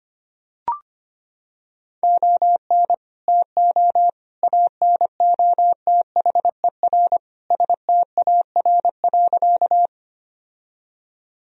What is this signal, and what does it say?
Morse code, heard as a keyed tone, about 25 wpm, 700 Hz. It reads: ON TO ANOTHER STAR.
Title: Sentences from Top 1000 Words - Review Part 2 - 25wpm